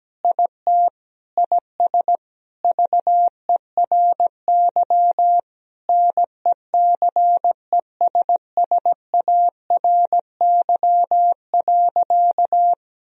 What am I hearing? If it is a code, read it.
Morse code, heard as a keyed tone, about 17 wpm, 700 Hz. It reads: IT IS VERY NECESSARY.
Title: Sentences from Top 1000 Words - Review Part 3 - 17wpm